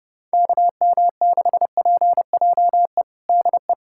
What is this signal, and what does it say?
Morse code, heard as a keyed tone, about 32 wpm, 700 Hz. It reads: KM6PJE DE
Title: USA Callsign DE Callsign 30wpm